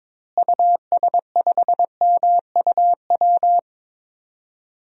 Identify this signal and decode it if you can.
Morse code, heard as a keyed tone, about 22 wpm, 700 Hz. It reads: US5MUW
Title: DXCC-100 Call Signs - 6 Characters - Repeated 3x - 22wpm